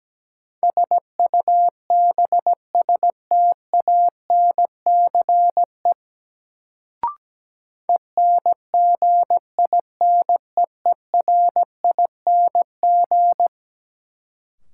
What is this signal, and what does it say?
Morse code, heard as a keyed tone, about 17 wpm, 700 Hz. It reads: SUBSTANCE ENGINEERING
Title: Sentences from Top 1500 Words - New Word Drill - 17wpm